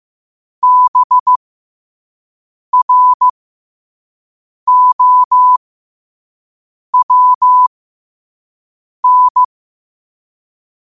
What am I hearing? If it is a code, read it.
Morse code, heard as a keyed tone, about 15 wpm, spaced as for 5 wpm, 1000 Hz. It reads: BROWN
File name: qotdfiles / 5wpm